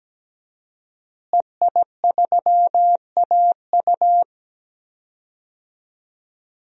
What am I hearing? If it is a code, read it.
Morse code, heard as a keyed tone, about 17 wpm, 700 Hz. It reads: EI3AU
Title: DXCC-100 Call Signs - 5 Characters - Repeated 2x - 17wpm